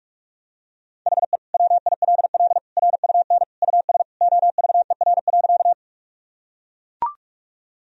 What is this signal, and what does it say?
Morse code, heard as a keyed tone, about 42 wpm, 700 Hz. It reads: HE WILL RUN US OVER.
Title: Sentences from Top 1000 Words - Review Part 1 - 45wpm